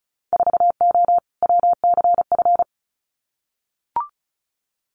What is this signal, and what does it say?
Morse code, heard as a keyed tone, about 34 wpm, 700 Hz. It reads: 4O WCF
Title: Winter Field Day Exchanges Encore 35wpm